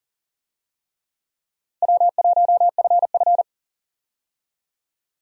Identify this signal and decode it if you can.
Morse code, heard as a keyed tone, about 37 wpm, 700 Hz. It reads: W1FF
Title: USA Call Signs - 4 Characters - 40wpm